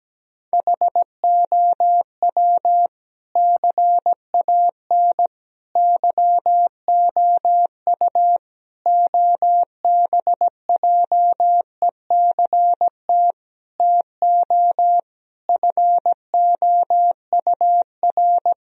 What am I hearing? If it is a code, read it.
Morse code, heard as a keyed tone, about 17 wpm, 700 Hz. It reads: HOW CAN YOU OBJECT TO FOUR
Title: Sentences from Top 1000 Words - Review Part 3 - 17wpm